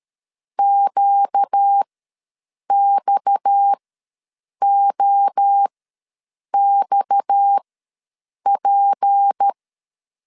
Morse code, heard as a keyed tone, about 13 wpm, spaced as for 7 wpm, 800 Hz. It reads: QXOXP
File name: LEZIONI / 6 PKX0 / 6 PKX ws 12wpm 3106